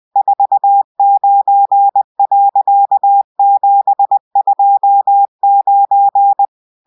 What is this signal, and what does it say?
Morse code, heard as a keyed tone, about 20 wpm, 800 Hz. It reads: 49.729